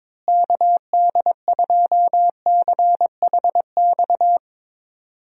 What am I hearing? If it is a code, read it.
Morse code, heard as a keyed tone, about 21 wpm, 700 Hz. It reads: KD2CHX
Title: USA Call Signs - 6 Characters - Repeated 2x - 22wpm